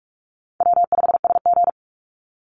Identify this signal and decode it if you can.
Morse code, heard as a keyed tone, about 44 wpm, 700 Hz. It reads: W5SG
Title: Call Signs Repeated 3x - 45wpm